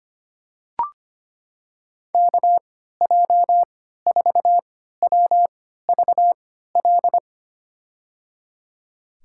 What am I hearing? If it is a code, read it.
Morse code, heard as a keyed tone, about 25 wpm, spaced as for 14 wpm, 700 Hz. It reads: KJ4WVL
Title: USA Call Signs - 6 Characters - 3x Character Spacing - 25wpm